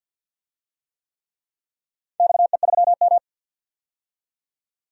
Morse code, heard as a keyed tone, about 46 wpm, 700 Hz. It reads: XE3M
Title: DXCC-100 Call Signs - 4 Characters - 50wpm